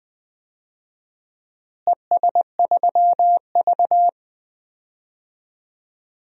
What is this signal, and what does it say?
Morse code, heard as a keyed tone, about 20 wpm, 700 Hz. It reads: ES3V
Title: DXCC-100 Call Signs - 4 Characters - Repeated 2x - 20wpm